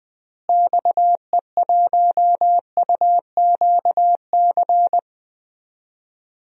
Morse code, heard as a keyed tone, about 20 wpm, 700 Hz. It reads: XE1UQC
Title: DXCC-100 Call Signs - 6 Characters - Repeated 3x - 20wpm